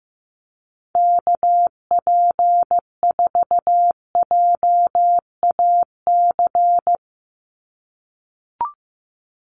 Morse code, heard as a keyed tone, about 15 wpm, 700 Hz. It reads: KP4JAC
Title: USA Call Signs 15wpm